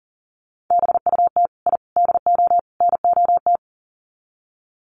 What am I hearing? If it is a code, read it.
Morse code, heard as a keyed tone, about 40 wpm, 700 Hz. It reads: BUT I DO NOT
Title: Sets of 4 Words 40wpm